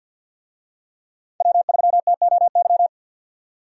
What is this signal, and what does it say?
Morse code, heard as a keyed tone, about 48 wpm, 700 Hz. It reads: W3TOY
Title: USA Call Signs - 5 Characters - 50wpm